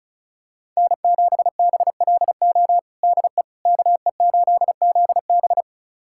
Morse code, heard as a keyed tone, about 36 wpm, 700 Hz. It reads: N7BLO DE KE8ZB
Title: USA Callsign DE Callsign - 5 Characters - 35wpm